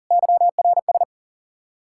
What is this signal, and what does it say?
Morse code, heard as a keyed tone, about 51 wpm, 700 Hz. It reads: YRS